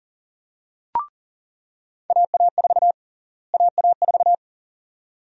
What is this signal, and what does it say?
Morse code, heard as a keyed tone, about 39 wpm, 700 Hz. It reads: AA4 AA4
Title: USA Call Sign Prefixes - Repeated 2x - 40wpm